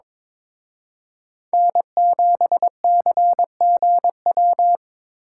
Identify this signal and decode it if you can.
Morse code, heard as a keyed tone, about 26 wpm, 700 Hz. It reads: N7CGW
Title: Speed Racing - USA 1x3 Call Signs 22wpm